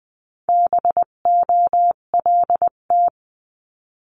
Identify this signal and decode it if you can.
Morse code, heard as a keyed tone, about 20 wpm, 700 Hz. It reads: BOLT
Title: 258 Lesson 42c - Words 16-20wpm